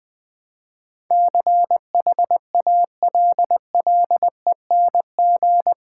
Morse code, heard as a keyed tone, about 20 wpm, 700 Hz. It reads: CHALLENG
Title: Build A Word - 4 Letters At A Time - 11-Letter Words - 20wpm